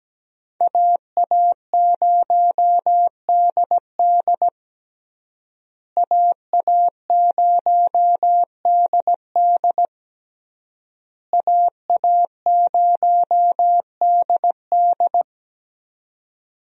Morse code, heard as a keyed tone, about 17 wpm, 700 Hz. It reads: AA0DD AA0DD AA0DD
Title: USA Call Signs - 5 Characters - Repeated 3x - 17wpm